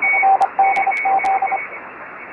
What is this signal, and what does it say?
Morse code, heard as a keyed tone, about 25 wpm, 780 Hz, with radio noise. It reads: NDZ
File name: RS44 UA9FDZ